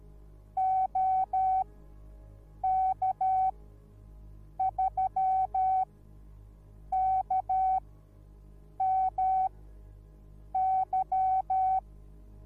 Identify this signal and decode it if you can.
Morse code, heard as a keyed tone, about 13 wpm, spaced as for 6 wpm, 760 Hz. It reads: OK3KMY